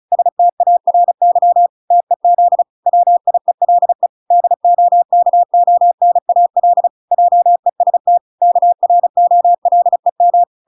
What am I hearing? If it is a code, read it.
Morse code, heard as a keyed tone, about 35 wpm, 700 Hz. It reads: STARY TEZ WIELE DOKONAL JEST KROLEM